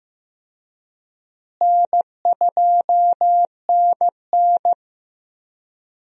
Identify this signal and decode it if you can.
Morse code, heard as a keyed tone, about 16 wpm, 700 Hz. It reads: N2NN
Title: USA Call Signs - 4 Characters - Repeated 3x - 15wpm